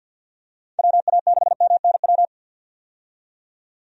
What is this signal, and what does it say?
Morse code, heard as a keyed tone, about 52 wpm, 700 Hz. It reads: WA6MNW
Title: USA Call Signs - 6 Characters - 50wpm